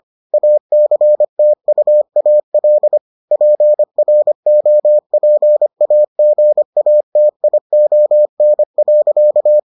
Morse code, heard as a keyed tone, about 25 wpm, 600 Hz. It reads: ACTUAL PROPAGATION.